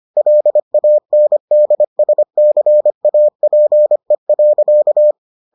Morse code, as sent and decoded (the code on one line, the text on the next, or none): .-.. .- -. -.. ... -.-. .- .--. .-.-.-
LANDSCAPE.